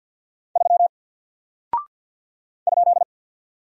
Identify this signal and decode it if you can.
Morse code, heard as a keyed tone, about 48 wpm, 700 Hz. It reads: 3 ?
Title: Single Character (Letter:Number:Punctuation:Prosign) Alt 50wpm